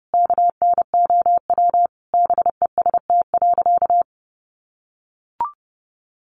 Morse code, heard as a keyed tone, about 30 wpm, 700 Hz. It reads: KNOW BEST.
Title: Sentences from Top 600 Words 30wpm